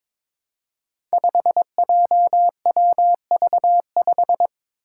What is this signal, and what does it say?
Morse code, heard as a keyed tone, about 21 wpm, 700 Hz. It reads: 5JWV5
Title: Five Challenging Letter-Number Combinations 22wpm